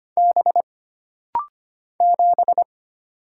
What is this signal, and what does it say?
Morse code, heard as a keyed tone, about 25 wpm, 700 Hz. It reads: B 7
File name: Single Character (Letter:Number:Punctuation:Prosign) - Rapid-Fire - 25wpm